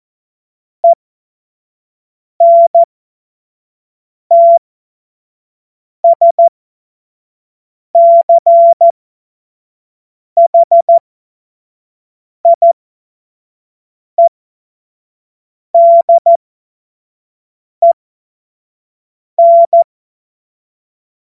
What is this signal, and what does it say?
Morse code, heard as a keyed tone, about 14 wpm, spaced as for 5 wpm, 680 Hz. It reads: ENTSCHIEDEN
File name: Morsekurs / Witze / Witz 26. Oktober 2021, 25 F70